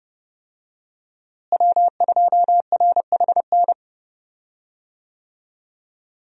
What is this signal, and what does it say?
Morse code, heard as a keyed tone, about 31 wpm, 700 Hz. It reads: W2RHN
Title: USA Call Signs - 5 Characters - Repeated 2x - 30wpm